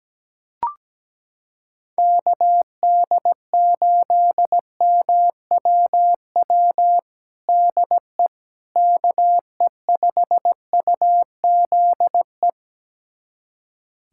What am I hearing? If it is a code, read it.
Morse code, heard as a keyed tone, about 17 wpm, 700 Hz. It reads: KD8MWW DE KE5UZE
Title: USA Callsign DE Callsign - 6 Characters - 17wpm